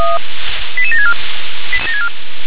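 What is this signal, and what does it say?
Five-tone selective call (ZVEI), heard as quick runs of five tones, tones of about 70 ms. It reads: 88764
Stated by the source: ZVEI five-tone selective call received by SDR.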